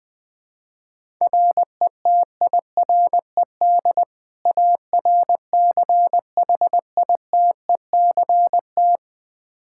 Morse code, heard as a keyed tone, about 20 wpm, 700 Hz. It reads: RETIRED ARCHITECT
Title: Practice QSO - Exchange 3 - More Info - 20wpm